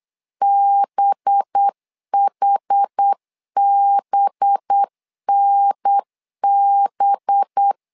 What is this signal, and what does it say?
Morse code, heard as a keyed tone, about 8 wpm, 800 Hz. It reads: BHBNB
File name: LEZIONI / 5 LBVC / 5 LB 8wpm 2816